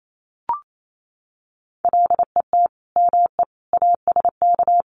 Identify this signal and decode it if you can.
Morse code, heard as a keyed tone, about 28 wpm, 700 Hz. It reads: LET ME ASK
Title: Sets of 4 Words 28wpm